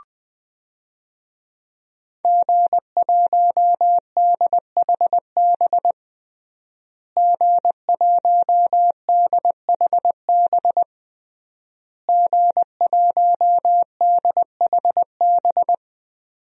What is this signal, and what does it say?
Morse code, heard as a keyed tone, about 20 wpm, 700 Hz. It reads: G1DHB G1DHB G1DHB
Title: DXCC-100 Call Signs - 5 Characters - Repeated 3x - 20wpm